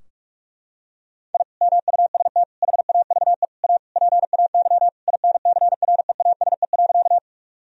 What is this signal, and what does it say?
Morse code, heard as a keyed tone, about 46 wpm, 700 Hz. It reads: I MUST HAVE A PAY INCREASE.
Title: Sentences from Top 1000 Words - Review Part 4 - 45wpm